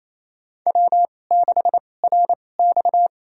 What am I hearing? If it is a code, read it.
Morse code, heard as a keyed tone, about 28 wpm, spaced as for 20 wpm, 700 Hz. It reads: W6RX
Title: USA Call Signs - 4 Characters - 2x Character Spacing - 28wpm